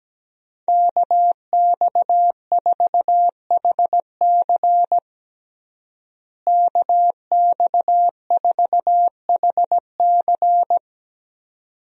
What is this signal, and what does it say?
Morse code, heard as a keyed tone, about 17 wpm, 700 Hz. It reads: KX4HC KX4HC
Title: USA Call Signs - 5 Characters - Repeated 2x - 17wpm